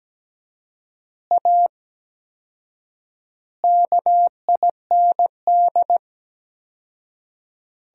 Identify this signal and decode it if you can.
Morse code, heard as a keyed tone, about 17 wpm, 700 Hz. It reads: A KIND
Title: 4x Word-Spacing - Sentences from Top 500 Words - 17wpm